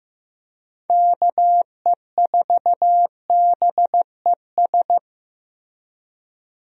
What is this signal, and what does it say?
Morse code, heard as a keyed tone, about 15 wpm, 700 Hz. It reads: KE4BES
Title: USA Call Signs - 6 Characters - Repeated 2x - 15wpm